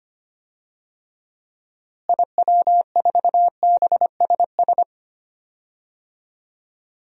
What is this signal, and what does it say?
Morse code, heard as a keyed tone, about 25 wpm, 700 Hz. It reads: IW4BSS
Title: DXCC-100 Call Signs - 6 Characters - Repeated 3x - 25wpm